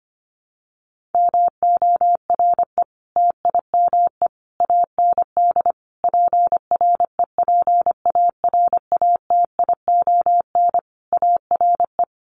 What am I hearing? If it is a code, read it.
Morse code, heard as a keyed tone, about 25 wpm, 700 Hz. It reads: MORE TIME AND PREPARATION ARE
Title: Sentences from Top 1400 Words 25wpm